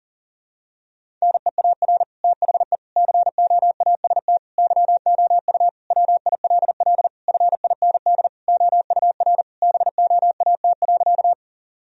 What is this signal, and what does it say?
Morse code, heard as a keyed tone, about 37 wpm, 700 Hz. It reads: NEAR THE COAST YOU WILL FIND OUR BOAT.
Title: Sentences from Top 1000 Words - Review Part 4 - 40wpm